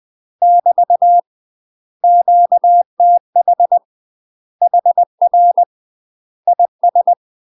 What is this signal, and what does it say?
Morse code, heard as a keyed tone, about 20 wpm, 700 Hz. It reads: <BT> QTH HR IS
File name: UK MASTER QSOs TRAINING / 20WPM / UK MASTER QSOs 0045